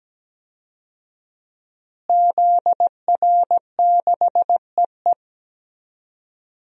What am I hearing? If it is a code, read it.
Morse code, heard as a keyed tone, about 17 wpm, 700 Hz. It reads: ZR6EE
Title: DXCC-100 Call Signs - 5 Characters - Repeated 2x - 17wpm